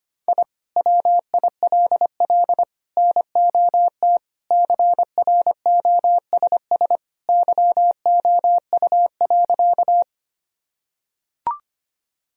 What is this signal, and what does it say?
Morse code, heard as a keyed tone, about 25 wpm, 700 Hz. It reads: I WILL NOT CROSS YOU.
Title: Sentences from Top 1000 Words - Review Part 1 - 25wpm